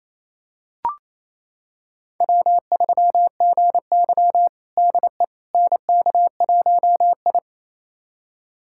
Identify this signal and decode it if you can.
Morse code, heard as a keyed tone, about 28 wpm, 700 Hz. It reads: W3GY DE NK1I
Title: USA Callsign DE Callsign - 4 Characters - 28wpm